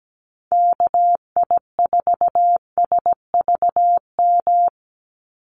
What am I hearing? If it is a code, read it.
Morse code, heard as a keyed tone, about 17 wpm, 700 Hz. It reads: KI4SVM
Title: NA SOTA Activator Callsigns 17wpm